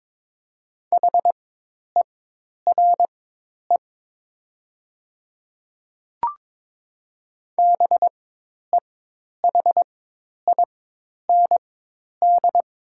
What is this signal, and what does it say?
Morse code, heard as a keyed tone, about 22 wpm, spaced as for 10 wpm, 700 Hz. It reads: HERE BEHIND
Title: 4x Character Spacing - Top 500 Words - 22wpm